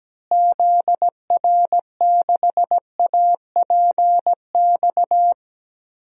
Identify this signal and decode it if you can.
Morse code, heard as a keyed tone, about 17 wpm, 700 Hz. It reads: ZR6APX
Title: DXCC-100 Call Signs - 6 Characters - Repeated 2x - 17wpm